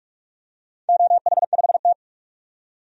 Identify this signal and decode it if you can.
Morse code, heard as a keyed tone, about 44 wpm, 690 Hz. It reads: OH5T
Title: DXCC-100 Call Signs - 4 Characters - 45wpm